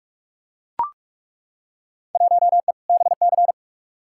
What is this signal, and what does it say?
Morse code, heard as a keyed tone, about 43 wpm, 700 Hz. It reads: 1E BC
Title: Field Day Exchanges - 45wpm